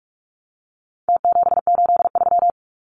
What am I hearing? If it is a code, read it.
Morse code, heard as a keyed tone, about 44 wpm, 700 Hz. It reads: T783
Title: Numbers - Contest Style - 4-Digit 45wpm